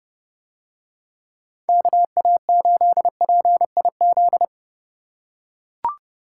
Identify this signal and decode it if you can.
Morse code, heard as a keyed tone, about 29 wpm, 700 Hz. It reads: KA8PIZ